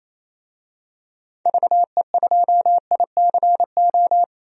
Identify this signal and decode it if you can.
Morse code, heard as a keyed tone, about 28 wpm, 700 Hz. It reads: VE2ICO
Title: DXCC-100 Call Signs - 6 Characters - 28wpm